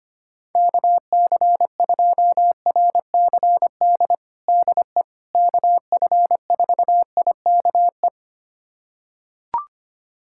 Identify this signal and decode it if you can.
Morse code, heard as a keyed tone, about 25 wpm, 700 Hz. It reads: KC2RCD DE KF4IKE
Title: USA Callsign DE Callsign 25wpm